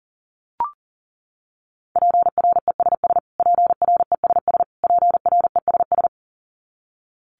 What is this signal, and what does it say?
Morse code, heard as a keyed tone, about 40 wpm, 700 Hz. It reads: PRESS PRESS PRESS
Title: Top 300 Words Repeated 40wpm